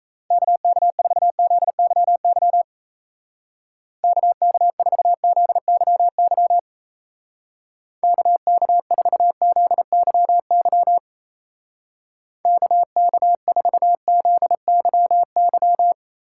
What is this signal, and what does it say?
Morse code, heard as a keyed tone, about 35 wpm, 700 Hz. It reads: KK4ZYY KK4ZYY KK4ZYY KK4ZYY
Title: Speed Racing - USA 2x3 Call Signs 28wpm